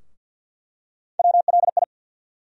Morse code, heard as a keyed tone, about 47 wpm, 700 Hz. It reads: WLI
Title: USA Call Sign Postfixes - 3 Letters 50wpm